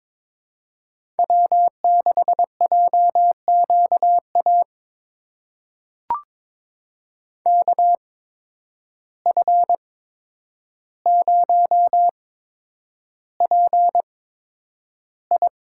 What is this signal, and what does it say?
Morse code, heard as a keyed tone, about 21 wpm, spaced as for 9 wpm, 700 Hz. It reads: W6JQA KF0PI
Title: USA Call Signs - 8x Character Spacing - 22wpm